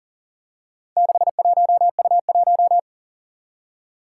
Morse code, heard as a keyed tone, about 40 wpm, 700 Hz. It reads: 61U1
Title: Four Challenging Letter-Number Combinations 40wpm